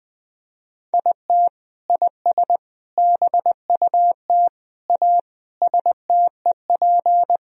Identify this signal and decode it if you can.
Morse code, heard as a keyed tone, about 20 wpm, 700 Hz. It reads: IT IS BUT A STEP
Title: Sentences from Top 1000 Words - Review Part 2 - 20wpm